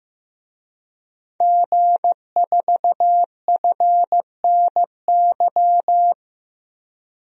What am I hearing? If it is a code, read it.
Morse code, heard as a keyed tone, about 15 wpm, 700 Hz. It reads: G4FNY
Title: DXCC-100 Call Signs - 5 Characters - 15wpm